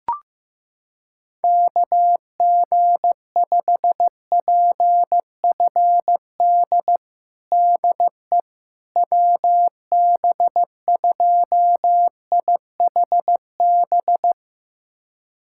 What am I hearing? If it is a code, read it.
Morse code, heard as a keyed tone, about 15 wpm, 700 Hz. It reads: KG5PFD DE WB2IHB
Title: USA Callsign DE Callsign 15wpm